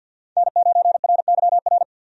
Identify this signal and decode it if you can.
Morse code, heard as a keyed tone, about 46 wpm, 700 Hz. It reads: N9RYR
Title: USA Call Signs - 5 Characters - 50wpm